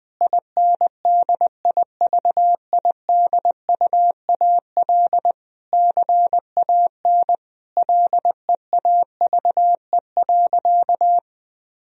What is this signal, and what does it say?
Morse code, heard as a keyed tone, about 20 wpm, 700 Hz. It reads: INDIVIDUAL CAN LEAVE.
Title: Sentences from Top 1000 Words - Review Part 4 - 20wpm